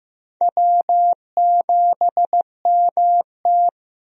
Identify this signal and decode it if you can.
Morse code, heard as a keyed tone, about 15 wpm, 700 Hz. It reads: W7MT
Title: USA Call Signs - 4 Characters - Repeated 2x - 15wpm